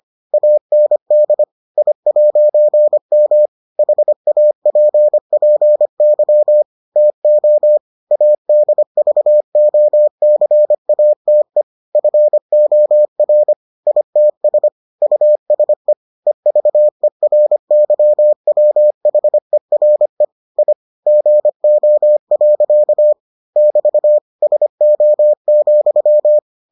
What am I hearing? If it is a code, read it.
Morse code, heard as a keyed tone, about 25 wpm, 600 Hz. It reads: AND I'M HAPPY TO ADVOCATE FOR ITS USE EVERYWHERE I GO. <BT> SO,